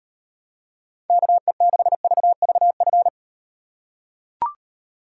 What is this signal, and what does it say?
Morse code, heard as a keyed tone, about 38 wpm, 700 Hz. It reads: KE6VVF